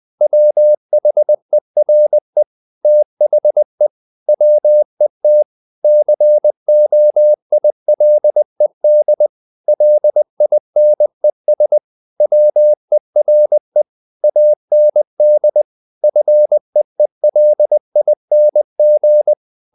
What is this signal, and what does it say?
Morse code, heard as a keyed tone, about 20 wpm, 600 Hz. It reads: WHERE THE WET COILED LINES WERE AND FEELING
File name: the old man and the sea / the old man and the sea part 1 20wpm